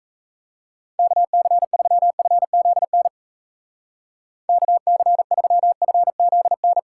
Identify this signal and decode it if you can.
Morse code, heard as a keyed tone, about 34 wpm, 700 Hz. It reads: KC3FZN KC3FZN
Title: Speed Racing - USA 2x3 Call Signs 28wpm